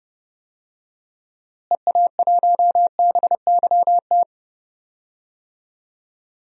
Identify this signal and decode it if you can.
Morse code, heard as a keyed tone, about 30 wpm, 700 Hz. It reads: EA1BYT
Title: DXCC-100 Call Signs - 6 Characters - Repeated 3x - 30wpm